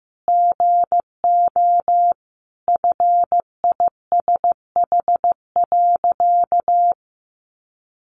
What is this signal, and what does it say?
Morse code, heard as a keyed tone, about 15 wpm, 700 Hz. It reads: GO FISH.
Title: Sentences from Top 1400 Words 15wpm